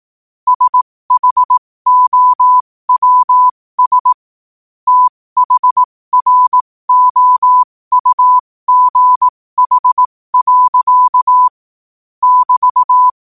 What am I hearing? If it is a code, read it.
Morse code, heard as a keyed tone, about 18 wpm, spaced as for 15 wpm, 1000 Hz. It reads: SHOWS THROUGH. <BT>
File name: qotdfiles / 15wpm